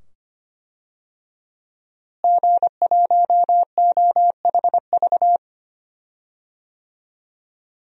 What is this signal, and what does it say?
Morse code, heard as a keyed tone, about 25 wpm, 700 Hz. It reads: G1OHV
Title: DXCC-100 Call Signs - 5 Characters - Repeated 2x - 25wpm